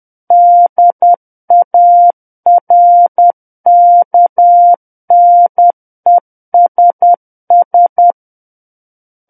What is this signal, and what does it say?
Morse code, heard as a keyed tone, about 10 wpm, 700 Hz. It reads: DARKNESS
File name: Book One / 01-Chapter 1 - 10 WPM